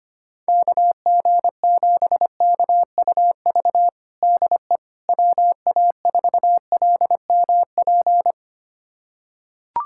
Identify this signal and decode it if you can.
Morse code, heard as a keyed tone, about 25 wpm, 700 Hz. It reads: KG7KUV DE WA4LMP
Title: USA Callsign DE Callsign - 6 Characters - 25wpm